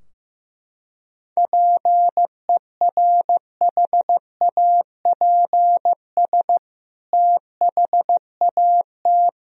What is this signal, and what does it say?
Morse code, heard as a keyed tone, about 15 wpm, 700 Hz. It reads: PERHAPS THAT
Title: Sentences from Top 1000 Words - Review Part 4 - 15wpm